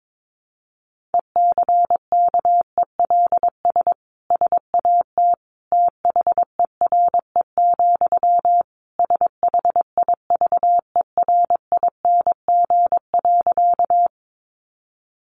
Morse code, heard as a keyed tone, about 22 wpm, 700 Hz. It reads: ECKELS SAT THERE, SHIVERING.
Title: A Sound Of Thunder 22wpm